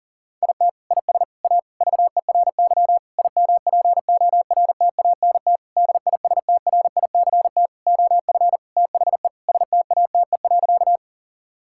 Morse code, heard as a keyed tone, about 40 wpm, 700 Hz. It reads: IT IS A VERY IMPORTANT DISTRICT OF THE STATE.
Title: Sentences from Top 1600 Words 40wpm